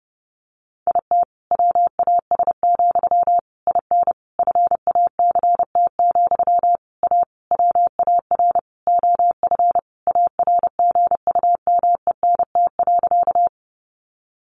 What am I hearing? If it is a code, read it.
Morse code, heard as a keyed tone, about 30 wpm, 700 Hz. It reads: IT WAS, IN FACT, A WAR OF ARGUMENT.
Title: Sentences from Top 1400 Words 30wpm